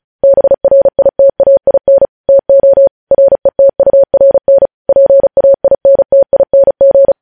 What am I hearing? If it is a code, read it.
Morse code, heard as a keyed tone, about 36 wpm, 550 Hz. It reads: BRITAIN TO RETURN PAINTING
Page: News headlines in Morse code at 35 wpm